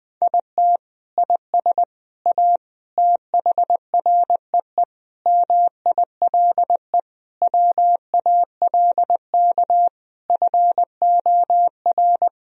Morse code, heard as a keyed tone, about 20 wpm, 700 Hz. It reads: IT IS A THREE MILE WALK FOR
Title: Sentences from Top 1000 Words - Review Part 1 - 20wpm